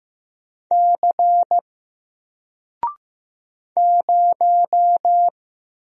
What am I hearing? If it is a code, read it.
Morse code, heard as a keyed tone, about 15 wpm, 700 Hz. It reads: C 0
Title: Single Letter-Number Alt 15wpm